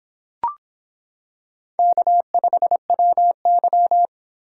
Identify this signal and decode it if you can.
Morse code, heard as a keyed tone, about 26 wpm, 700 Hz. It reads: K5WY